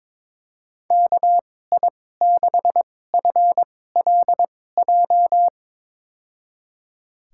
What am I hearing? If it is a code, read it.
Morse code, heard as a keyed tone, about 22 wpm, spaced as for 16 wpm, 700 Hz. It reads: KI6FLJ
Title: USA Call Signs - 6 Characters - 2x Character Spacing - 22wpm